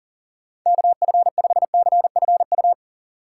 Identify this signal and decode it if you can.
Morse code, heard as a keyed tone, about 40 wpm, 700 Hz. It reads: KF5CFU
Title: USA Call Signs - 6 Characters - 6x Character Spacing - 40wpm